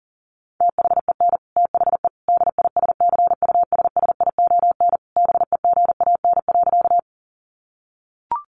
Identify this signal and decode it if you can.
Morse code, heard as a keyed tone, about 40 wpm, 700 Hz. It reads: THEN THE DISCUSSION BEGAN.
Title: Sentences from Top 1100 Words 40wpm